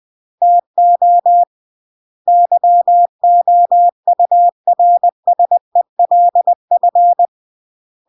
Morse code, heard as a keyed tone, about 20 wpm, 700 Hz. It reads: TO YOURSELF